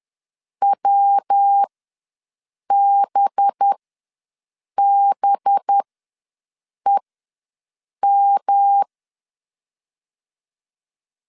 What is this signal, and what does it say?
Morse code, heard as a keyed tone, about 10 wpm, spaced as for 6 wpm, 800 Hz. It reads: WBBEM
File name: LEZIONI / 5 LBVC / 5 LB ws 10wpm 3616